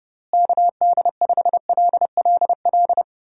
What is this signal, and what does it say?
Morse code, heard as a keyed tone, about 30 wpm, 700 Hz. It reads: KD5LLL
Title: USA Call Signs - 6 Characters - Repeated 3x - 30wpm